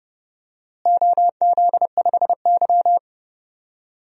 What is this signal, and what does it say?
Morse code, heard as a keyed tone, about 29 wpm, 700 Hz. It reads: OZ5Y
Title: DXCC-100 Call Signs - 4 Characters - Repeated 2x - 30wpm